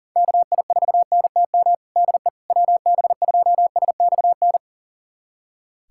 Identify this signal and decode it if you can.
Morse code, heard as a keyed tone, about 40 wpm, 700 Hz. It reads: KI4NTM DE WB2SXN
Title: USA Callsign DE Callsign 40wpm